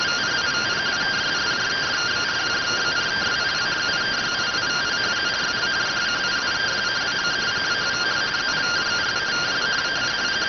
Two-tone teleprinter signal, about 170 Hz shift.